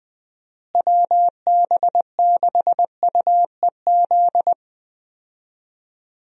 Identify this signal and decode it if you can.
Morse code, heard as a keyed tone, about 20 wpm, 700 Hz. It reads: WB6UEZ
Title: USA Call Signs - 6 Characters - Repeated 2x - 20wpm